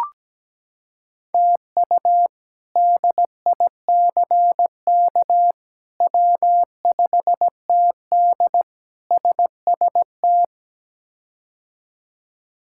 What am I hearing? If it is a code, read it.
Morse code, heard as a keyed tone, about 17 wpm, 700 Hz. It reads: TU DICK W5TD SST
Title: K1USN SST QSO Practice - 17wpm